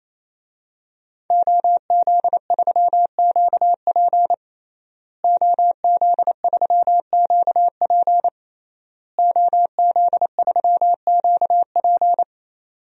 Morse code, heard as a keyed tone, about 28 wpm, 700 Hz. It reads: OZ3QP OZ3QP OZ3QP
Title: DXCC-100 Call Signs - 5 Characters - Repeated 3x - 28wpm